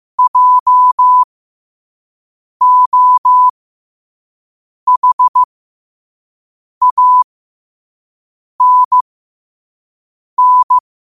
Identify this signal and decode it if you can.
Morse code, heard as a keyed tone, about 15 wpm, spaced as for 5 wpm, 1000 Hz. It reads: JOHANN